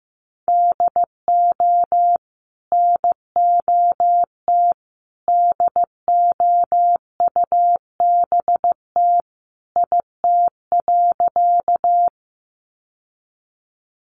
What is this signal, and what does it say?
Morse code, heard as a keyed tone, about 15 wpm, 700 Hz. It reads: DO NOT DOUBT IT.
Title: Sentences from Top 1500 Words 15wpm